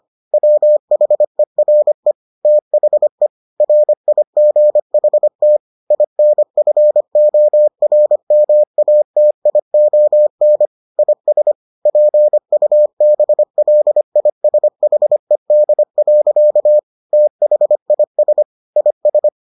Morse code, heard as a keyed tone, about 25 wpm, 600 Hz. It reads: WHERE THE RIGHT INFORMATION IS PUBLISHED. THIS IS